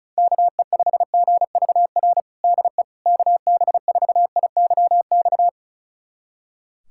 Morse code, heard as a keyed tone, about 35 wpm, 700 Hz. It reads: KE5GVR DE KB4IYX
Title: USA Callsign DE Callsign 35wpm